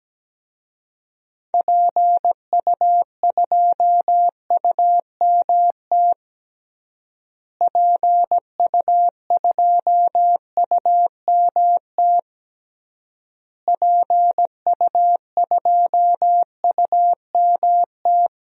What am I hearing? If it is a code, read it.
Morse code, heard as a keyed tone, about 17 wpm, 700 Hz. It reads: PU2UMT PU2UMT PU2UMT
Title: DXCC-100 Call Signs - 6 Characters - Repeated 3x - 17wpm